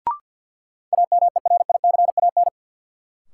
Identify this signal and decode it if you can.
Morse code, heard as a keyed tone, about 52 wpm, 700 Hz. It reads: AMERICAN
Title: Sets of 2 Words - New Word Drill - 50wpm